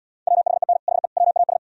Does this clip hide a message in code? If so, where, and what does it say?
Yes: Morse code from 0.3 to 1.6 s, heard as a keyed tone, about 77 wpm, 700 Hz. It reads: WHEN HE WAS